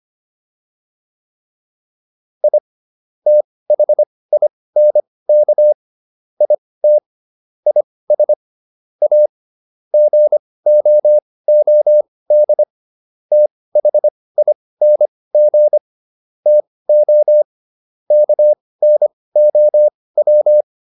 Morse code, heard as a keyed tone, about 25 wpm, spaced as for 18 wpm, 600 Hz. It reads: I THINK IT IS A GOOD THING TO KNOW